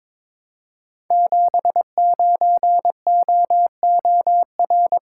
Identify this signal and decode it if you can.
Morse code, heard as a keyed tone, about 22 wpm, 700 Hz. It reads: 79OOR
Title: Code Groups - 5 Letter:Numbers - 22wpm